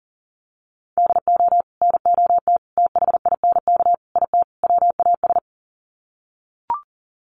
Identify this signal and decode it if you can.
Morse code, heard as a keyed tone, about 40 wpm, 700 Hz. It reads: DO NOT THINK IT WAS
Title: Sets of 5 Words Encore 40wpm